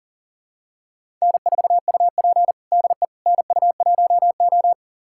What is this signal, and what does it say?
Morse code, heard as a keyed tone, about 40 wpm, 700 Hz. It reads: N4UP DE NU1O
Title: USA Callsign DE Callsign - 4 Characters - 40wpm